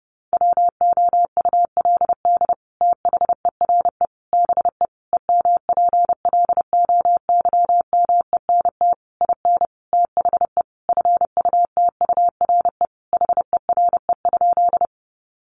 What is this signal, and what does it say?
Morse code, heard as a keyed tone, about 30 wpm, 700 Hz. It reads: WOULD THERE BE EMPLOYMENT IN THE FUTURE HERE?
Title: Sentences from Top 1500 Words 30wpm